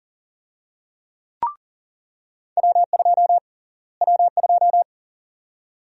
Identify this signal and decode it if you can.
Morse code, heard as a keyed tone, about 40 wpm, 700 Hz. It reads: W2 W2
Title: USA Call Sign Prefixes - Repeated 2x - 40wpm